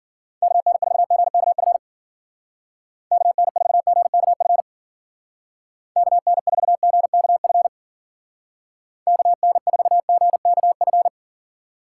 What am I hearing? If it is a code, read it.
Morse code, heard as a keyed tone, about 51 wpm, 700 Hz. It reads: KN4GKF KN4GKF KN4GKF KN4GKF